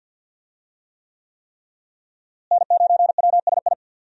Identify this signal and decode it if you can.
Morse code, heard as a keyed tone, about 46 wpm, 700 Hz. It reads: N9WSI
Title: USA Call Signs - 5 Characters - Repeated 3x - 50wpm